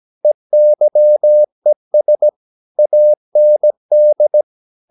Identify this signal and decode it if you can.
Morse code, heard as a keyed tone, about 17 wpm, 600 Hz. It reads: EYES AND
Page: Romanen Never af Ken Follet på engelsk.